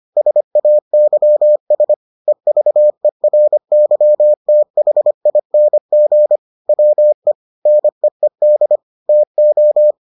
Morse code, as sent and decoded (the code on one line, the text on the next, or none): ... .- -.-- ... / . ...- . .-. -.-- - .... .. -. --. / .-- . / -. . -.. / - ---
SAYS EVERYTHING WE NEED TO